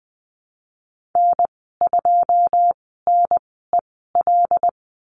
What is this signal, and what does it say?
Morse code, heard as a keyed tone, about 20 wpm, spaced as for 15 wpm, 700 Hz. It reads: N2NEL
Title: 154 Lesson 25g - Letter p - Call Signs 20wpm